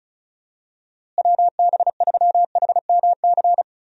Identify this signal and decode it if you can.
Morse code, heard as a keyed tone, about 36 wpm, 700 Hz. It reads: WB3HMC
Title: USA Call Signs - 6 Characters - Repeated 2x - 35wpm